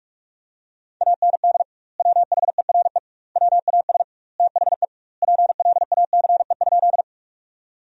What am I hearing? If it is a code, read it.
Morse code, heard as a keyed tone, about 46 wpm, 700 Hz. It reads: AND WHERE WAS THE PLACE?
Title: 2x Word-Spacing - Sentences from Top 200 Words - 45wpm